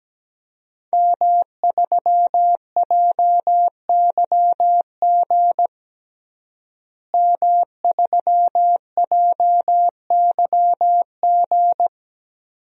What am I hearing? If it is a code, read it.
Morse code, heard as a keyed tone, about 17 wpm, 700 Hz. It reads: M3JYG M3JYG
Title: DXCC-100 Call Signs - 5 Characters - Repeated 3x - 17wpm